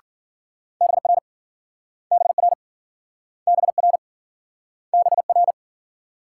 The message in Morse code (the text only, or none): BR BR BR BR